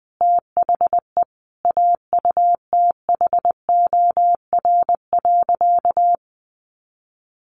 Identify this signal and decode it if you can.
Morse code, heard as a keyed tone, about 20 wpm, 700 Hz. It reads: THE AUTHOR.
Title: Sentences from Top 1200 Words 20wpm